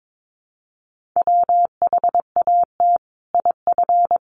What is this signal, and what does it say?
Morse code, heard as a keyed tone, about 22 wpm, 700 Hz. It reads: WHAT IF
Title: Sentences from Top 1400 Words 22wpm